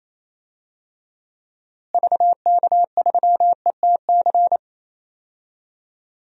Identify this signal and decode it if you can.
Morse code, heard as a keyed tone, about 28 wpm, 700 Hz. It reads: VK3ETC
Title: DXCC-100 Call Signs - 6 Characters - Repeated 3x - 28wpm